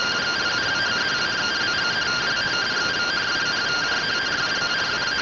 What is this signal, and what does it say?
Two-tone teleprinter signal, about 170 Hz shift.